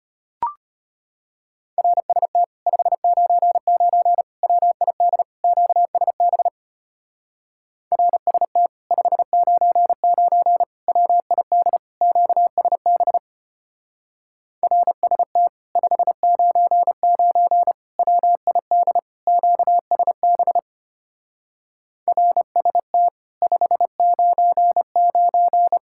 Morse code, heard as a keyed tone, about 31 wpm, 700 Hz. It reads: RST 599 WID QSB RST 599 WID QSB RST 599 WID QSB RST 599
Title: Speed-Racing Common QSO Phrases - 25wpm